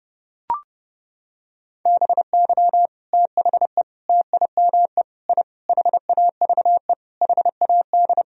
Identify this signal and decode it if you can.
Morse code, heard as a keyed tone, about 30 wpm, 700 Hz. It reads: BY THE TIME I HAVE HAD
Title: Sentences from Top 1000 Words - Review Part 1 - 30wpm